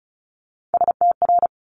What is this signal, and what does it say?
Morse code, heard as a keyed tone, about 36 wpm, 700 Hz. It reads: STR
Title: Multi-letter Phonemes Quick Fire 35wpm